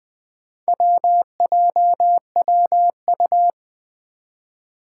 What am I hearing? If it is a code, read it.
Morse code, heard as a keyed tone, about 20 wpm, 700 Hz. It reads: WJWU
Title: Four Challenging Letter-Number Combinations 20wpm